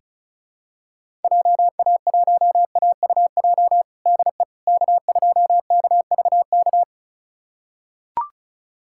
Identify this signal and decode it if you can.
Morse code, heard as a keyed tone, about 34 wpm, 700 Hz. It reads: JA1AUJ DE K2KVK
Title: USA Callsign DE Callsign 35wpm